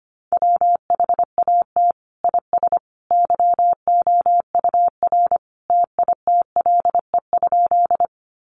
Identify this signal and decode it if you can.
Morse code, heard as a keyed tone, about 25 wpm, 700 Hz. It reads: WHAT IS YOUR TITLE?
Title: Sentences from Top 1200 Words 25wpm